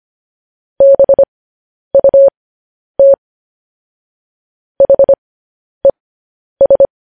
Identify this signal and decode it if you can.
Morse code, heard as a keyed tone, about 25 wpm, spaced as for 10 wpm, 550 Hz. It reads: BUT HES
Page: news headlines in morse code at 10 wpm